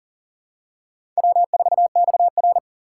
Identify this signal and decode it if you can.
Morse code, heard as a keyed tone, about 40 wpm, 700 Hz. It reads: W4XR
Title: USA Call Signs - 2x Character Spacing - 40wpm